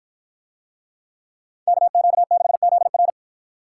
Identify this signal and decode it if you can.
Morse code, heard as a keyed tone, about 53 wpm, 700 Hz. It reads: KQ6ZR